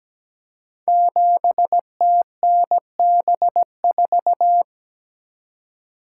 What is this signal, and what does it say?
Morse code, heard as a keyed tone, about 17 wpm, 700 Hz. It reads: 7TNB4
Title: Code Groups - 5 Letter:Numbers - 17wpm